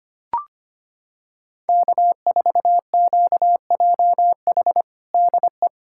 Morse code, heard as a keyed tone, about 25 wpm, 700 Hz. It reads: K4QJH DE
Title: Practice QSO - Exchange 4 - Ending - 25wpm